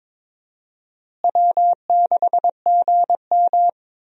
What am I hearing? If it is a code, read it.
Morse code, heard as a keyed tone, about 22 wpm, 700 Hz. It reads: W6GM
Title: USA Call Signs - 4 Characters - Repeated 2x - 22wpm